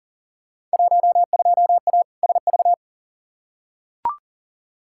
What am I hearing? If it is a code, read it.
Morse code, heard as a keyed tone, about 39 wpm, 700 Hz. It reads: 12A SV
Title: Field Day Exchanges Encore - 40wpm